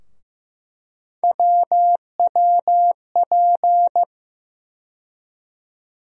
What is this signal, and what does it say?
Morse code, heard as a keyed tone, about 15 wpm, 700 Hz. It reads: WWP
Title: USA Call Sign Postfixes - 3 Letters 15wpm